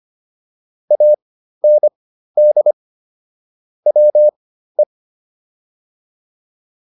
Morse code, heard as a keyed tone, about 25 wpm, spaced as for 13 wpm, 600 Hz. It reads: AND WE